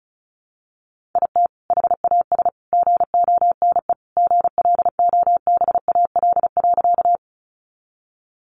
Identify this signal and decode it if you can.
Morse code, heard as a keyed tone, about 34 wpm, 700 Hz. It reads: IT HAS GONE GLOBAL.
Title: Sentences from Top 1200 Words 35wpm